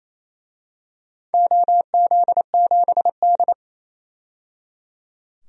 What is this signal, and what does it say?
Morse code, heard as a keyed tone, about 28 wpm, 700 Hz. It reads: OZ7D
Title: DXCC-100 Call Signs - 4 Characters - 28wpm